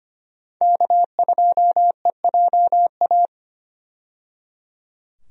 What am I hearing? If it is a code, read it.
Morse code, heard as a keyed tone, about 25 wpm, 700 Hz. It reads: K2EJA
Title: USA Call Signs - 5 Characters - 25wpm